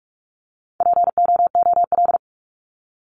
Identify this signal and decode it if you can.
Morse code, heard as a keyed tone, about 44 wpm, 700 Hz. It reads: POOL
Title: Top 1751-2000 Words 45wpm